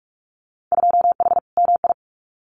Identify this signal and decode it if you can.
Morse code, heard as a keyed tone, about 46 wpm, 700 Hz. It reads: 2H MI
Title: Winter Field Day Exchanges Encore 45wpm